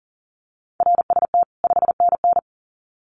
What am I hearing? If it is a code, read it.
Morse code, heard as a keyed tone, about 40 wpm, 700 Hz. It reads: RST 5NN
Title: Common QSO Phrases 40wpm